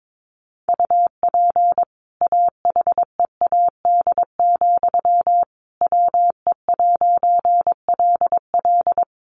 Morse code, heard as a keyed tone, about 22 wpm, 700 Hz. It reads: UP AHEAD, WE'LL
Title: A Sound Of Thunder 22wpm